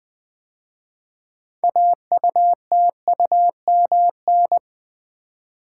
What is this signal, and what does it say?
Morse code, heard as a keyed tone, about 20 wpm, 700 Hz. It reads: AUTUMN
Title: 6 Letter Words - First 3 Letters Given - 20wpm